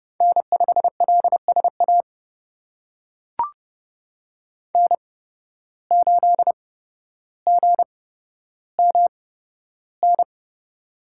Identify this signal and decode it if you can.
Morse code, heard as a keyed tone, about 30 wpm, spaced as for 13 wpm, 700 Hz. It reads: N5LSA N8GMN
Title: USA Call Signs - 5 Characters - 8x Character Spacing - 30wpm